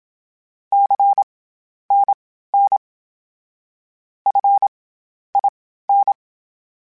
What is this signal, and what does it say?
Morse code, heard as a keyed tone, about 27 wpm, spaced as for 12 wpm, 800 Hz. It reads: CNN FIN
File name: CW Comparisons / CW Comparisons-mp3s / CW Comparisons Broken Characters @25